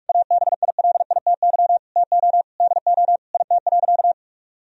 Morse code, heard as a keyed tone, about 46 wpm, 700 Hz. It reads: ABILITY TO DO IT.
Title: Sentences from Top 1000 Words - Review Part 3 - 45wpm